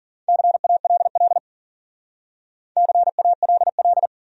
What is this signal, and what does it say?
Morse code, heard as a keyed tone, about 52 wpm, 700 Hz. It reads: CALL CALL